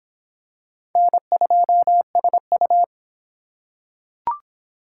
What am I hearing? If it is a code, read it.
Morse code, heard as a keyed tone, about 25 wpm, 700 Hz. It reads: N2SU